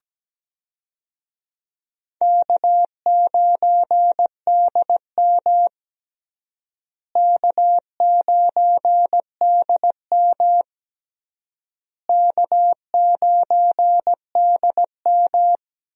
Morse code, heard as a keyed tone, about 17 wpm, 700 Hz. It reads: K9DM K9DM K9DM
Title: USA Call Signs - 4 Characters - Repeated 3x - 17wpm